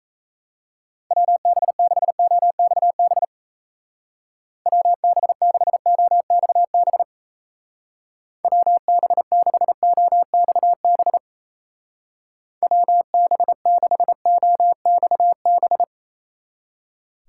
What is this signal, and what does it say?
Morse code, heard as a keyed tone, about 35 wpm, 700 Hz. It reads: WB6OXB WB6OXB WB6OXB WB6OXB
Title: Speed Racing - USA 2x3 Call Signs 28wpm